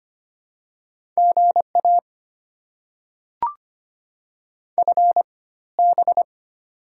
Morse code, heard as a keyed tone, about 25 wpm, spaced as for 16 wpm, 700 Hz. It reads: GA FB
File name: Top 20 QSO Elements - 4x Character Spacing - 25wpm